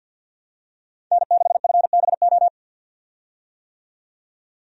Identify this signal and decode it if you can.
Morse code, heard as a keyed tone, about 47 wpm, 700 Hz. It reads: N6FBO
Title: USA Call Signs - 5 Characters - 50wpm